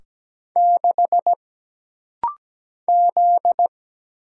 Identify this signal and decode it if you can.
Morse code, heard as a keyed tone, about 17 wpm, 700 Hz. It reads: 6 Z
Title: Single Letter-Number - Rapid-Fire 17wpm